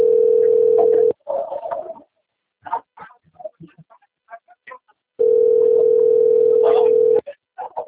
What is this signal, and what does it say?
Ringback tone (North America), 440 + 480 Hz.
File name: out- - - - - .21243